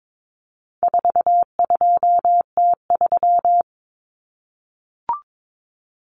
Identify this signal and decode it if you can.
Morse code, heard as a keyed tone, about 22 wpm, 700 Hz. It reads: 42T3
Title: Common Cut Numbers - 4-Digit 22wpm